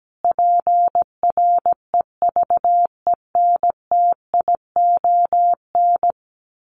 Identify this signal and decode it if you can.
Morse code, heard as a keyed tone, about 17 wpm, 700 Hz. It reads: PREVENTION
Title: Sets of 5 Words Encore2 17wpm